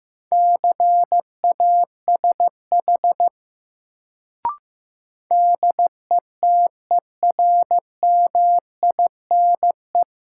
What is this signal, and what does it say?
Morse code, heard as a keyed tone, about 15 wpm, 700 Hz. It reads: CASH DETERMINE
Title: Sentences from Top 700 Words - New Word Drill - 15wpm